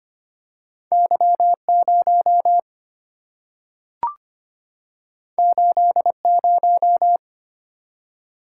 Morse code, heard as a keyed tone, about 25 wpm, 700 Hz. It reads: Y0 80
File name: Two Letter-Number Combinations - 25wpm